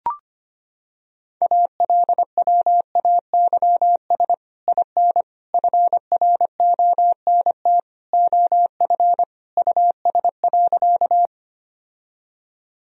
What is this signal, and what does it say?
Morse code, heard as a keyed tone, about 25 wpm, 700 Hz. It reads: ALWAYS IN FRONT OF US.
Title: Sentences from Top 1000 Words - Review Part 2 - 25wpm